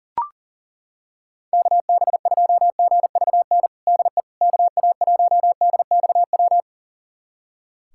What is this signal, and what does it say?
Morse code, heard as a keyed tone, about 37 wpm, 700 Hz. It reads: KB2GVN DE KA1DXW
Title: USA Callsign DE Callsign - 6 Characters - 40wpm